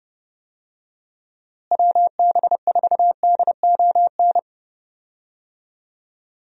Morse code, heard as a keyed tone, about 29 wpm, 700 Hz. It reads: WB4DON
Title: USA Call Signs - 6 Characters - Repeated 2x - 30wpm